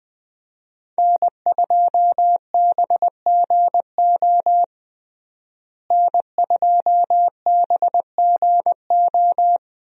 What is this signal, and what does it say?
Morse code, heard as a keyed tone, about 20 wpm, 700 Hz. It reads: N2BGO N2BGO
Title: USA Call Signs - 5 Characters - Repeated 3x - 20wpm